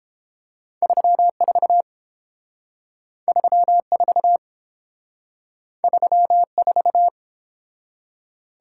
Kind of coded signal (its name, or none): Morse code